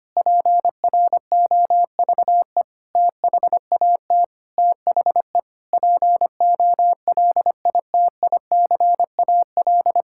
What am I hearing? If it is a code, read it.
Morse code, heard as a keyed tone, about 25 wpm, 700 Hz. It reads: PROVE THAT THE POLITICAL
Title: Sentences from Top 1600 Words 25wpm